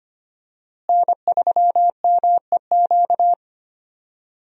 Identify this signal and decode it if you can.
Morse code, heard as a keyed tone, about 25 wpm, 700 Hz. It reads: N3MEQ
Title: USA Call Signs - 5 Characters - Repeated 2x - 25wpm